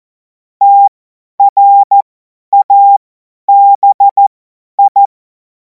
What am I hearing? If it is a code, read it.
Morse code, heard as a keyed tone, about 14 wpm, spaced as for 10 wpm, 800 Hz. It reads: TRABI
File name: Morsekurs / Witze / Witz 23. Juni 2026 50 F70